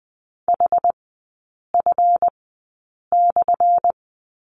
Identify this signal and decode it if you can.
Morse code, heard as a keyed tone, about 20 wpm, spaced as for 8 wpm, 700 Hz. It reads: HF/
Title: 178 Lesson 29e - Slash - Three Characters 20wpm